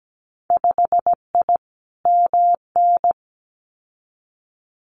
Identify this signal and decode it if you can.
Morse code, heard as a keyed tone, about 17 wpm, 700 Hz. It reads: 5I MN
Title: Winter Field Day Exchanges Encore 17wpm